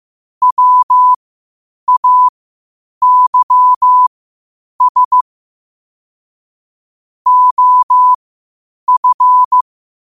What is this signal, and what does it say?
Morse code, heard as a keyed tone, about 15 wpm, spaced as for 8 wpm, 1000 Hz. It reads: WAYS OF
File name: qotdfiles / 8wpm